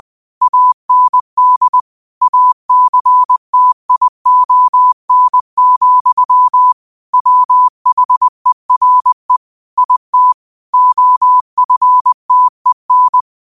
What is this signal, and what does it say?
Morse code, heard as a keyed tone, about 20 wpm, 1000 Hz. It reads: AND ACTION, WHERE IT OFTEN